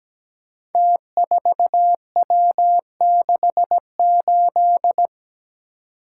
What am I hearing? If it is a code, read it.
Morse code, heard as a keyed tone, about 17 wpm, 700 Hz. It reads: T4W68
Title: Code Groups - 5 Letter:Numbers - 17wpm